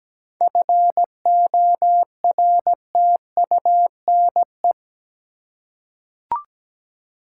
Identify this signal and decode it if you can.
Morse code, heard as a keyed tone, about 17 wpm, 700 Hz. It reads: FORTUNE
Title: Seven Letter Words - 17wpm